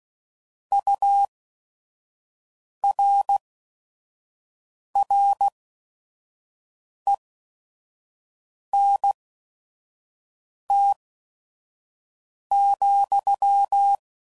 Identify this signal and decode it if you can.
Morse code, heard as a keyed tone, about 16 wpm, 800 Hz. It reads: U R R E N T ,